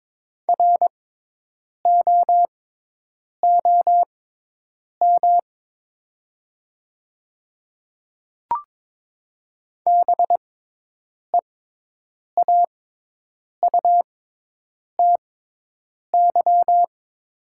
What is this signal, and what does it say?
Morse code, heard as a keyed tone, about 22 wpm, spaced as for 8 wpm, 700 Hz. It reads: ROOM BEAUTY
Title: 6x Character Spacing - Top 500 Words - 22wpm